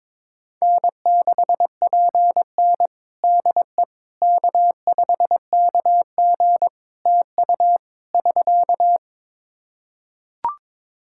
Morse code, heard as a keyed tone, about 22 wpm, 700 Hz. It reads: N6PN DE K5KG TU <SK>
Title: Practice QSO - Exchange 4 - Ending - 22wpm